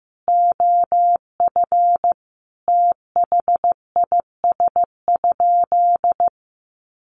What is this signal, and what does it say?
Morse code, heard as a keyed tone, about 15 wpm, 700 Hz. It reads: OF THIS?
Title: Sentences from Top 1100 Words 15wpm